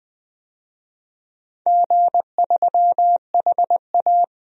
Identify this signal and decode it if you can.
Morse code, heard as a keyed tone, about 20 wpm, 700 Hz. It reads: G3HA
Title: DXCC-100 Call Signs - 4 Characters - 20wpm